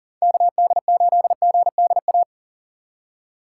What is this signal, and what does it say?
Morse code, heard as a keyed tone, about 50 wpm, 700 Hz. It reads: KD8GDA
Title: Speed Racing - USA Call Signs 40wpm